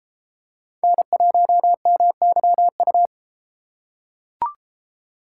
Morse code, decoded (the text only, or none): N1MYU